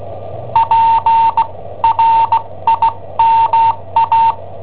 Morse code, heard as a keyed tone, about 17 wpm, spaced as for 14 wpm, 910 Hz. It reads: PRIMA
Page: Cassetta esercizi preparata da I7OHP per IW5ELP